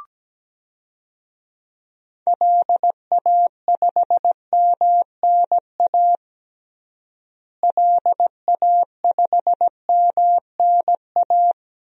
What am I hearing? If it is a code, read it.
Morse code, heard as a keyed tone, about 17 wpm, 700 Hz. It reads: LA5MNA LA5MNA
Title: DXCC-100 Call Signs - 6 Characters - Repeated 2x - 17wpm